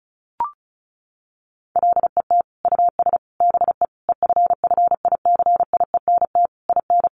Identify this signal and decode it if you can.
Morse code, heard as a keyed tone, about 34 wpm, 700 Hz. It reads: LET US BE EFFICIENT IN
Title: Sentences from Top 1300 Words 35wpm